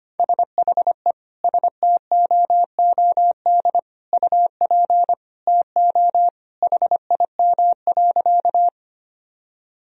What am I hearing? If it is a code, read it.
Morse code, heard as a keyed tone, about 25 wpm, 700 Hz. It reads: SHE STOOD UP TO HIM.
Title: Sentences from Top 1000 Words - Review Part 2 - 25wpm